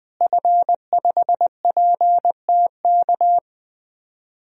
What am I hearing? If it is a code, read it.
Morse code, heard as a keyed tone, about 20 wpm, 700 Hz. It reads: F5PTK
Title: DXCC-100 Call Signs - 5 Characters - Repeated 2x - 20wpm